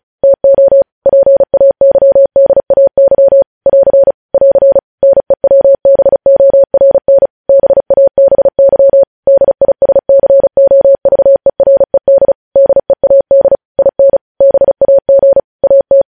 Morse code, heard as a keyed tone, about 36 wpm, 550 Hz. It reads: TO PAYDAY <AR> <AR> NEWBORN BABY DISCOVERED DEAD IN BAG AT
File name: podcast / News headlines-1x-35x35